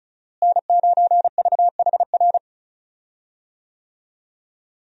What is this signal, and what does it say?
Morse code, heard as a keyed tone, about 36 wpm, 700 Hz. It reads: N9VHR
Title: USA Call Signs - 5 Characters - Repeated 3x - 35wpm